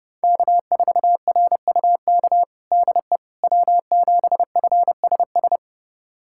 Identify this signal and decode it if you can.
Morse code, heard as a keyed tone, about 30 wpm, 700 Hz. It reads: K4RUK DE W7FSS
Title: USA Callsign DE Callsign - 5 Characters - 30wpm